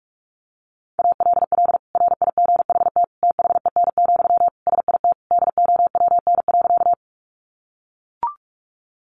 Morse code, heard as a keyed tone, about 44 wpm, 700 Hz. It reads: ALL RIGHT THEN, SIT DOWN.
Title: Sentences from Top 500 Words 45wpm